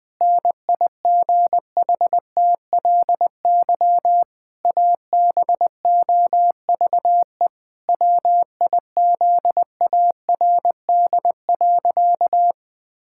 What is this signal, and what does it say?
Morse code, heard as a keyed tone, about 20 wpm, 700 Hz. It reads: NIGHTLY ABOVE WIZARD.
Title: Pangram Challenge v2 - 20wpm